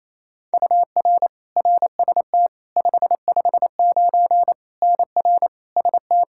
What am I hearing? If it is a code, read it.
Morse code, heard as a keyed tone, about 27 wpm, 700 Hz. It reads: UR RST 559 NR ST
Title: Practice QSO - Exchange 1 - Beginning - 28wpm